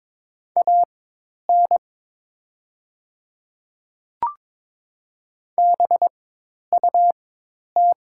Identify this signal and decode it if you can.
Morse code, heard as a keyed tone, about 22 wpm, spaced as for 10 wpm, 700 Hz. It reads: AN BUT
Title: 4x Character Spacing - Top 100 Words - 22wpm